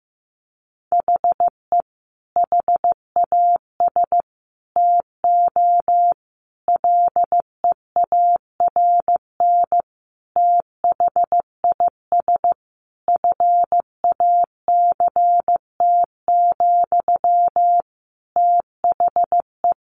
Morse code, heard as a keyed tone, about 15 wpm, 700 Hz. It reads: HE HAS TO LEARN THIS FACT, THE
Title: Sentences from Top 1200 Words 15wpm